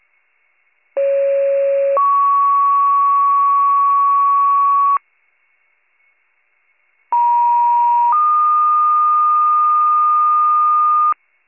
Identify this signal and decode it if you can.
Two-tone page, heard as a one-second tone, then a three-second tone, about 550 then 1120 Hz, about 950 then 1220 Hz. It is heard from 1.0 to 11.1 s.